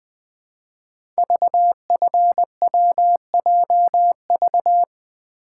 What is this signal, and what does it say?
Morse code, heard as a keyed tone, about 20 wpm, 700 Hz. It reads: VFWJV
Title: Code Groups - 5 Letters - 20wpm